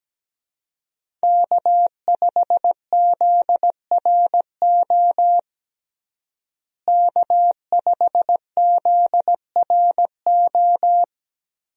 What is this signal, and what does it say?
Morse code, heard as a keyed tone, about 17 wpm, 700 Hz. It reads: K5ZRO K5ZRO
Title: USA Call Signs - 5 Characters - Repeated 3x - 17wpm